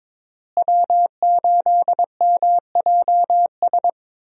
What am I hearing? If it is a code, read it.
Morse code, heard as a keyed tone, about 22 wpm, 700 Hz. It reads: W8MJS
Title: USA Call Signs - 5 Characters - Repeated 2x - 22wpm